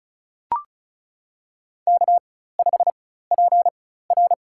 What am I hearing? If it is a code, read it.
Morse code, heard as a keyed tone, about 36 wpm, spaced as for 16 wpm, 700 Hz. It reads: K5PR
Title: USA Call Signs - 4x Character Spacing - 35wpm